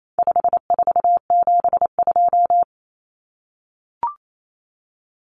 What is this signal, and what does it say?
Morse code, heard as a keyed tone, about 28 wpm, 700 Hz. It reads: 5472
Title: Common Cut Numbers - 4-Digit 28wpm